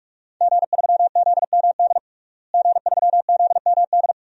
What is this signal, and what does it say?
Morse code, heard as a keyed tone, about 46 wpm, 700 Hz. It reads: G3ZMD G3ZMD
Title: DXCC-100 Call Signs - 5 Characters - Repeated 2x - 45wpm